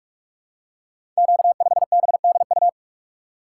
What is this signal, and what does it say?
Morse code, heard as a keyed tone, about 44 wpm, 700 Hz. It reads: Q5BDU
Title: Code Groups - 5 Letter:Numbers - 45wpm